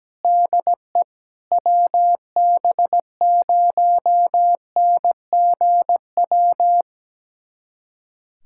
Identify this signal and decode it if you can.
Morse code, heard as a keyed tone, about 17 wpm, 700 Hz. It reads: DE WB0NGW
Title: USA Callsign DE Callsign - 6 Characters - 17wpm